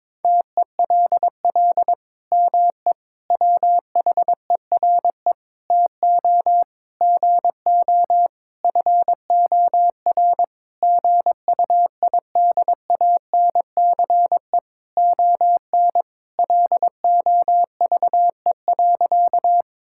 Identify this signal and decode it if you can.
Morse code, heard as a keyed tone, about 22 wpm, 700 Hz. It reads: TELL ME WHERE TO GO FOR GUIDANCE ON LOVE.
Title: Sentences from Top 1600 Words 22wpm